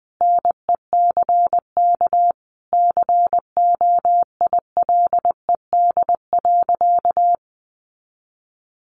Morse code, heard as a keyed tone, about 20 wpm, 700 Hz. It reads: NECK COILED.
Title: A Sound Of Thunder 20wpm